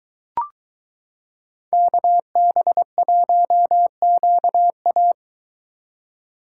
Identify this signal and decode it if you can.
Morse code, heard as a keyed tone, about 22 wpm, 700 Hz. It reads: KB1QA